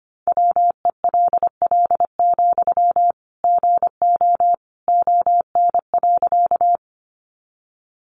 Morse code, heard as a keyed tone, about 25 wpm, 700 Hz. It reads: WELL, GO ON.
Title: Sentences from Top 200 Words 25wpm